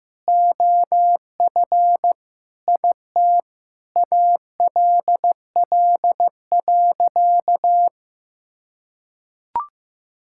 Morse code, heard as a keyed tone, about 16 wpm, 700 Hz. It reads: OF IT ALL.
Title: Sentences from Top 1000 Words - Review Part 6 - 15wpm